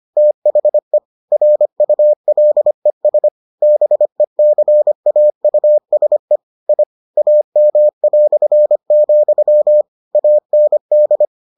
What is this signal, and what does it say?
Morse code, heard as a keyed tone, about 25 wpm, 600 Hz. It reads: THE RULES BECAUSE I AM", AND